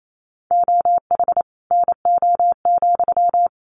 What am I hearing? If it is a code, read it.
Morse code, heard as a keyed tone, about 28 wpm, 700 Hz. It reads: OH NO,
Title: Sentences from Top 900 Words 28wpm